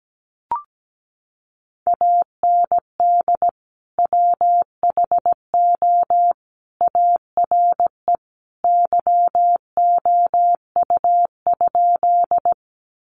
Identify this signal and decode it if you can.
Morse code, heard as a keyed tone, about 17 wpm, 700 Hz. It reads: AND WHO ARE YOU?
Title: Sentences from Top 100 Words 17wpm